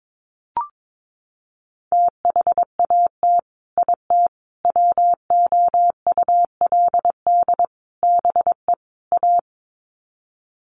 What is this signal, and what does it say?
Morse code, heard as a keyed tone, about 22 wpm, 700 Hz. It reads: THAT IT WOULD BE A
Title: Sets of 5 Words Encore2 22wpm